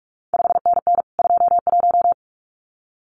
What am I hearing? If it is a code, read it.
Morse code, heard as a keyed tone, about 46 wpm, 700 Hz. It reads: 5NN 21
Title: CQ WW DX Contest Exchange Encore - 45wpm